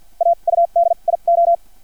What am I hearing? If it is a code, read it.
Morse code, heard as a keyed tone, about 50 wpm, spaced as for 38 wpm, 680 Hz. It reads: AUDIO
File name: archived / CRSnet